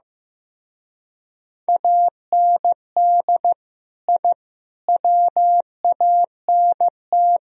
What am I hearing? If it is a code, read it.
Morse code, heard as a keyed tone, about 19 wpm, 700 Hz. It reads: AND I WANT